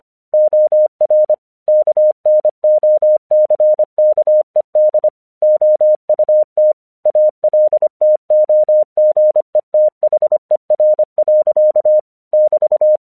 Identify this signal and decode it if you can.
Morse code, heard as a keyed tone, about 25 wpm, 610 Hz. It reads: OR KNOCKED OUT ALTOGETHER. <BT>
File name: 1 English 25wpm / AQC06